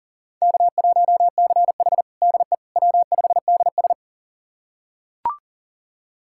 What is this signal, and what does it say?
Morse code, heard as a keyed tone, about 40 wpm, 700 Hz. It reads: K1CH DE W5DS
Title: USA Callsign DE Callsign - 4 Characters - 40wpm